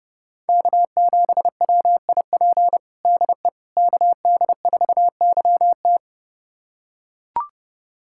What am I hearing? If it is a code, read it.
Morse code, heard as a keyed tone, about 30 wpm, 700 Hz. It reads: K7WIP DE KD4YT
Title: USA Callsign DE Callsign - 5 Characters - 30wpm